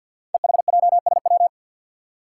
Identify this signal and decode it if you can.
Morse code, heard as a keyed tone, about 49 wpm, 700 Hz. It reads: EHJSW
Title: Code Groups - 5 Letters - 50wpm